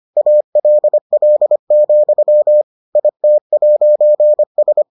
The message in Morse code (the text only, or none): ALL, IT'S